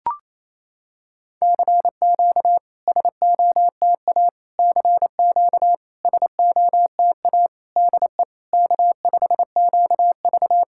Morse code, heard as a keyed tone, about 28 wpm, 700 Hz. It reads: CQ SOTA CQ SOTA DE K5QV
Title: SOTA QSO Practice 28wpm